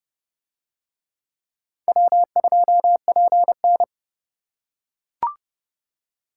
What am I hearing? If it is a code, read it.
Morse code, heard as a keyed tone, about 29 wpm, 700 Hz. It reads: W2PN